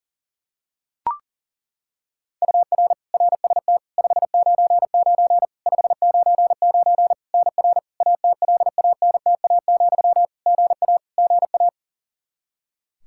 Morse code, heard as a keyed tone, about 40 wpm, 700 Hz. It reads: UR RST 599 599 NR ATLANTA, GA GA
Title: Practice QSO - Exchange 1 - Beginning - 40wpm